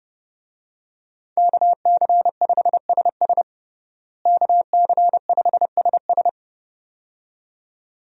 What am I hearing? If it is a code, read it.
Morse code, heard as a keyed tone, about 32 wpm, 700 Hz. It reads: KC5SS KC5SS
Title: USA Call Signs - 5 Characters - Repeated 2x - 30wpm